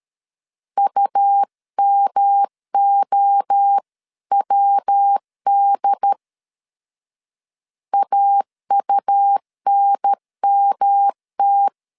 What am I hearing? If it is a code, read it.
Morse code, heard as a keyed tone, about 13 wpm, 800 Hz. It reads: UMOWD AUNMT
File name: CW / DET / 1 3 1500C S B 12wpm 2828